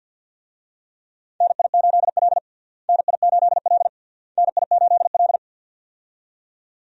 Morse code, heard as a keyed tone, about 52 wpm, 700 Hz. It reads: NI8L NI8L NI8L
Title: USA Call Signs - 4 Characters - Repeated 3x - 50wpm